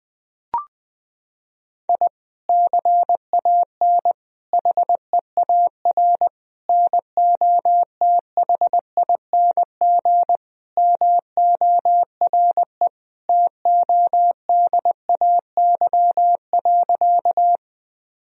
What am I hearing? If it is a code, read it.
Morse code, heard as a keyed tone, about 20 wpm, 700 Hz. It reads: I CAN HEAR NOTHING MORE TODAY.
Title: Sentences from Top 1000 Words - Review Part 4 - 20wpm